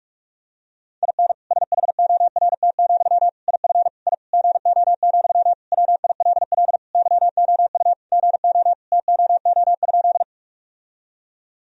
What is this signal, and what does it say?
Morse code, heard as a keyed tone, about 46 wpm, 700 Hz. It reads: IN SHORT, IF I GO, WILL YOU GO TOO?
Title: Sentences from Top 1000 Words - Review Part 2 - 45wpm